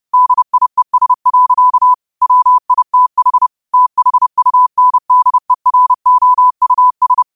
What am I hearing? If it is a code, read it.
Morse code, heard as a keyed tone, about 30 wpm, 1000 Hz. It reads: DIES. WITH THUNDEROUS
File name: qotdfiles / 30wpm